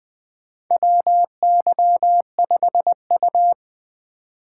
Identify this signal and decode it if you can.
Morse code, heard as a keyed tone, about 20 wpm, 700 Hz. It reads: WY5U
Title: USA Call Signs - 4x Character Spacing - 20wpm